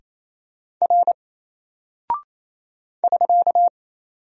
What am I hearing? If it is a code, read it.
Morse code, heard as a keyed tone, about 27 wpm, 700 Hz. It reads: R <SK>
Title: Single Character (Letter:Number:Punctuation:Prosign) 28wpm